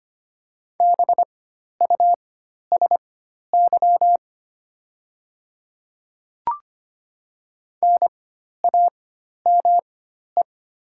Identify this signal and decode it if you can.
Morse code, heard as a keyed tone, about 25 wpm, spaced as for 12 wpm, 700 Hz. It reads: BUSY NAME
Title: 4x Character Spacing - Top 500 Words - 25wpm